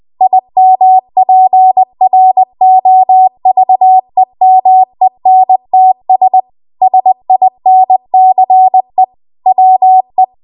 Morse code, heard as a keyed tone, about 20 wpm, 750 Hz. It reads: IMPROVEMENTS SINCE WE
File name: Morse / 20WPM